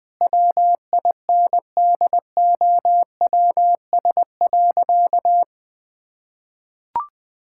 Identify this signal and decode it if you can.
Morse code, heard as a keyed tone, about 20 wpm, 700 Hz. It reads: WINDOWS.
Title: Pangram Challenge v2 - 20wpm